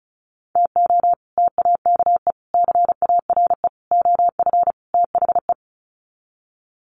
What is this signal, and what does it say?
Morse code, heard as a keyed tone, about 34 wpm, 700 Hz. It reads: TO TAKE CARE OF THE
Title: Sets of 5 Words Encore 35wpm